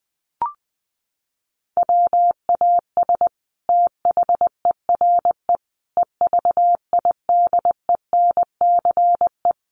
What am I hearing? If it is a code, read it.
Morse code, heard as a keyed tone, about 20 wpm, 700 Hz. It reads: WAS THERE EVIDENCE
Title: Sentences from Top 1300 Words 20wpm